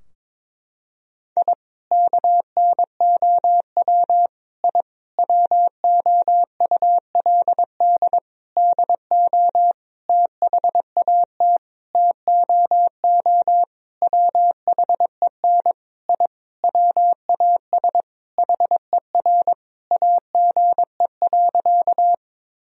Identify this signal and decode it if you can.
Morse code, heard as a keyed tone, about 22 wpm, 700 Hz. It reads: I KNOW I WOULD DO THAT TOO WHEN I WAS HER AGE.
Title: Sentences from Top 1000 Words - Review Part 3 - 22wpm